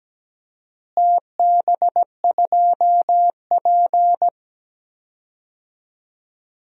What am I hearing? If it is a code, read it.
Morse code, heard as a keyed tone, about 17 wpm, 700 Hz. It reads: TB2P
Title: DXCC-100 Call Signs - 4 Characters - Repeated 2x - 17wpm